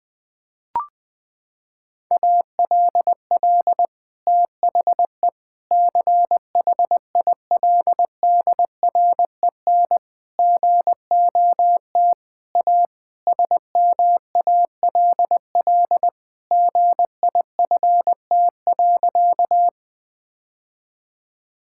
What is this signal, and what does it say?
Morse code, heard as a keyed tone, about 21 wpm, 700 Hz. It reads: ALL THE CHILDREN GOT A SMALL GIFT.
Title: Sentences from Top 1000 Words - Review Part 6 - 20wpm